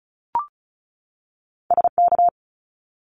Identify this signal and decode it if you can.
Morse code, heard as a keyed tone, about 36 wpm, 700 Hz. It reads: S K
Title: Top Two Letter Combinations in Words 35wpm